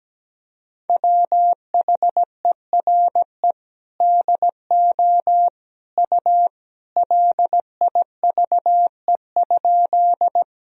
Morse code, heard as a keyed tone, about 17 wpm, 700 Hz. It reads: WHERE DO U LIVE?
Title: Sentences from Top 200 Words and Abbreviations 17wpm